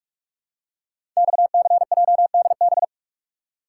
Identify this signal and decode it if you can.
Morse code, heard as a keyed tone, about 44 wpm, 690 Hz. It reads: XCJDB
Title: Code Groups - 5 Letters - 45wpm